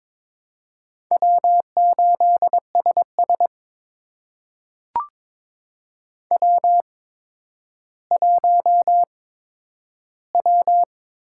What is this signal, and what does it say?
Morse code, heard as a keyed tone, about 22 wpm, spaced as for 11 wpm, 700 Hz. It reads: W8SS W1W
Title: USA Call Signs - 4 Characters - 8x Character Spacing - 22wpm